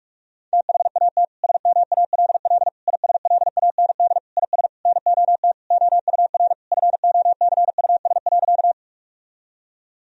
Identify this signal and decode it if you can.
Morse code, heard as a keyed tone, about 46 wpm, 700 Hz. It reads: THAT SMALL ISLAND IS NOT OUR FOCUS.
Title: Sentences from Top 1000 Words - Review Part 4 - 45wpm